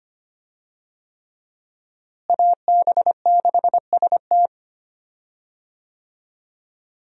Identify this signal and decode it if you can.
Morse code, heard as a keyed tone, about 25 wpm, 700 Hz. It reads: AB6ST
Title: USA Call Signs - 5 Characters - Repeated 3x - 25wpm